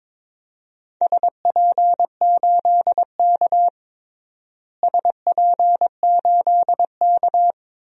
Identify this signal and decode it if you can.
Morse code, heard as a keyed tone, about 22 wpm, 700 Hz. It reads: SP8K SP8K
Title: DXCC-100 Call Signs - 4 Characters - Repeated 2x - 22wpm